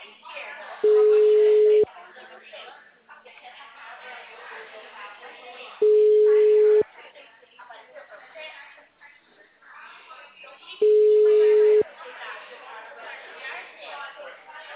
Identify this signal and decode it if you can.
Ringback tone (Europe, 425 Hz), with speech.